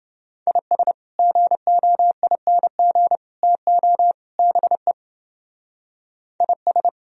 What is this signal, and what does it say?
Morse code, heard as a keyed tone, about 28 wpm, 700 Hz. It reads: IS GOING TO BE IS